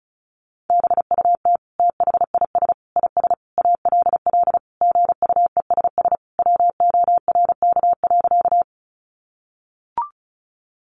Morse code, heard as a keyed tone, about 34 wpm, 700 Hz. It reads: BUT THIS IS ALL GUESS WORK.
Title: Sentences from Top 1500 Words 35wpm